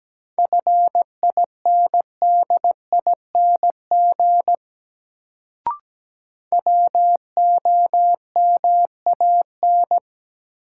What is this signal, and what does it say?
Morse code, heard as a keyed tone, about 17 wpm, 700 Hz. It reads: FINDING WOMAN
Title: Sentences from Top 1100 Words - New Word Drill Alt - 17wpm